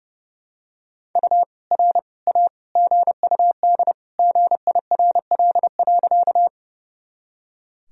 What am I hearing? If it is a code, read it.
Morse code, heard as a keyed tone, about 30 wpm, 700 Hz. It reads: U R A GUD GIRL.
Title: Sentences from Top 400 Words and Abbreviations 30wpm